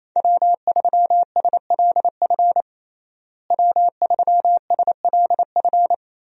Morse code, heard as a keyed tone, about 28 wpm, 700 Hz. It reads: W3SLF W3SLF
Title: USA Call Signs - 5 Characters - Repeated 3x - 28wpm